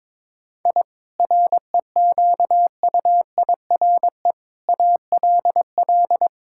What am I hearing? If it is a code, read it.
Morse code, heard as a keyed tone, about 22 wpm, 700 Hz. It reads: I REQUIRE ALL
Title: Sentences from Top 1000 Words - Review Part 6 - 22wpm